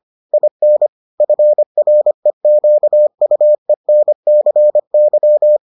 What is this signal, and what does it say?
Morse code, heard as a keyed tone, about 25 wpm, 600 Hz. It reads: IN FREQUENCY